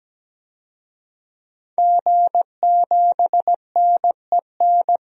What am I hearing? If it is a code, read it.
Morse code, heard as a keyed tone, about 17 wpm, 700 Hz. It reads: G7NEN
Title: DXCC-100 Call Signs - 5 Characters - 17wpm